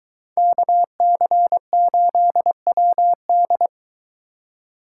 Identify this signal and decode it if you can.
Morse code, heard as a keyed tone, about 22 wpm, 700 Hz. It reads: KC8WD